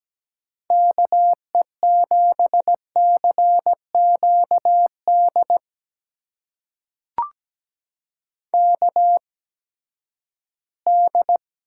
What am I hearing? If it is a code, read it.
Morse code, heard as a keyed tone, about 17 wpm, spaced as for 12 wpm, 700 Hz. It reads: KE7CQD KD
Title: USA Call Signs - 6 Characters - 8x Character Spacing - 17wpm